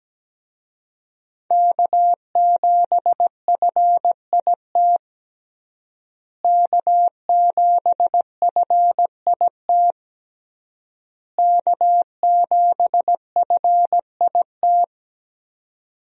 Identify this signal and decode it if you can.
Morse code, heard as a keyed tone, about 17 wpm, 700 Hz. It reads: K7FIT K7FIT K7FIT
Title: USA Call Signs - 5 Characters - Repeated 3x - 17wpm